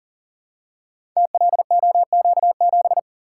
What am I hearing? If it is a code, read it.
Morse code, heard as a keyed tone, about 38 wpm, 690 Hz. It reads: TLOQ7
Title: Code Groups - 5 Letter:Numbers - 40wpm